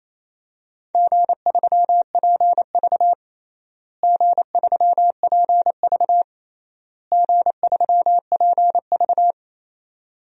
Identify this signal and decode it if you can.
Morse code, heard as a keyed tone, about 28 wpm, 700 Hz. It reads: G3PV G3PV G3PV
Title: DXCC-100 Call Signs - 4 Characters - Repeated 3x - 28wpm